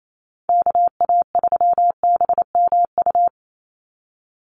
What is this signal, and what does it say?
Morse code, heard as a keyed tone, about 28 wpm, 700 Hz. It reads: KA3BMU
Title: Call Signs Repeated 2x - 28wpm